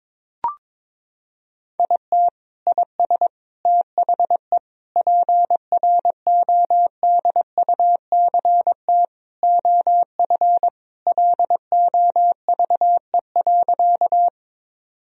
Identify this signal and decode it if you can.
Morse code, heard as a keyed tone, about 21 wpm, 700 Hz. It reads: IT IS THE PRODUCT OF LOVE.
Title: Sentences from Top 1000 Words - Review Part 2 - 22wpm